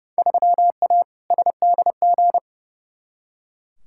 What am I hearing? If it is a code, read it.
Morse code, heard as a keyed tone, about 31 wpm, 700 Hz. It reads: 3A SDG
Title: Field Day Exchanges - 30wpm